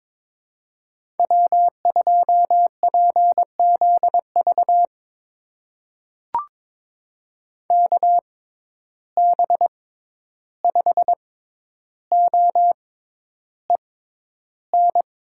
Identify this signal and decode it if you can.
Morse code, heard as a keyed tone, about 22 wpm, spaced as for 11 wpm, 700 Hz. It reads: W2PZV KB5OEN
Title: USA Call Signs - 6x Character Spacing - 22wpm